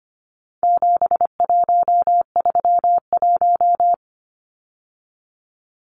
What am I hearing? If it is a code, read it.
Morse code, heard as a keyed tone, about 25 wpm, 700 Hz. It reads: 7131
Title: Numbers - 4-Digit 25wpm